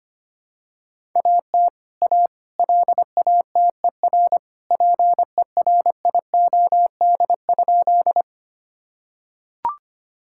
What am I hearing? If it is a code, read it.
Morse code, heard as a keyed tone, about 25 wpm, 700 Hz. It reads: AT A LATER PERIOD?
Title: Sentences from Top 1000 Words - Review Part 5 - 25wpm